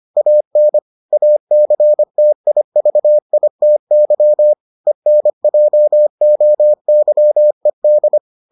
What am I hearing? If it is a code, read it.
Morse code, heard as a keyed tone, about 25 wpm, 600 Hz. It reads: AN ACTIVITY ENJOYED